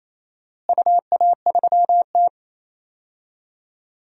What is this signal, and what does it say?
Morse code, heard as a keyed tone, about 28 wpm, 700 Hz. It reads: UA3T
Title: DXCC-100 Call Signs - 4 Characters - Repeated 2x - 28wpm